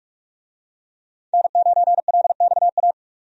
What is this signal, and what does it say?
Morse code, heard as a keyed tone, about 46 wpm, 700 Hz. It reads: N9LXA
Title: USA Call Signs - 5 Characters - 45wpm